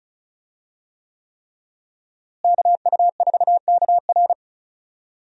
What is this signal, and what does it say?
Morse code, heard as a keyed tone, about 36 wpm, 700 Hz. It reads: KU4KR
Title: USA Call Signs - 5 Characters - Repeated 3x - 35wpm